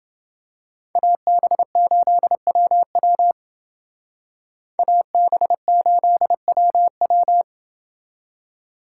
Morse code, heard as a keyed tone, about 29 wpm, 700 Hz. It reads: AB8WW AB8WW